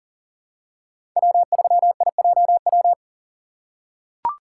Morse code, heard as a keyed tone, about 37 wpm, 700 Hz. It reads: W3IJW
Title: USA Call Signs - 5 Characters - 2x Character Spacing - 40wpm